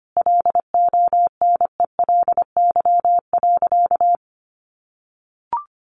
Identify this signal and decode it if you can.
Morse code, heard as a keyed tone, about 25 wpm, 700 Hz. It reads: LONELY.
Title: Easy Phrases 25wpm